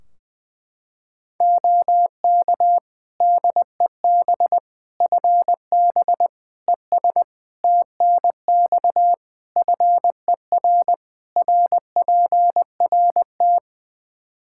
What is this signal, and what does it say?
Morse code, heard as a keyed tone, about 20 wpm, 700 Hz. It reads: OK DEB FB ES TNX FER RPRT
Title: Practice QSO - Exchange 2 - Info - 20wpm